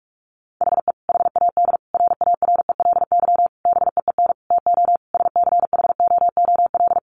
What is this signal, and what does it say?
Morse code, heard as a keyed tone, about 44 wpm, 700 Hz. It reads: HE HAD RARELY BEEN TO SCHOOL
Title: Sentences from Top 1400 Words 45wpm